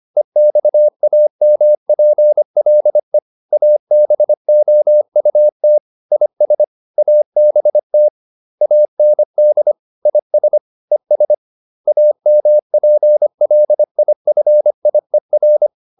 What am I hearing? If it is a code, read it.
Morse code, heard as a keyed tone, about 25 wpm, 600 Hz. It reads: EXAMPLE ABOUT IS ABT AND IS ES AMPLIFIER
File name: SS319 25